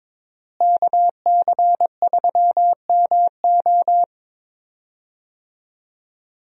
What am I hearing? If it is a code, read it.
Morse code, heard as a keyed tone, about 22 wpm, 700 Hz. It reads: KC3MO
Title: USA Call Signs - 5 Characters - Repeated 3x - 22wpm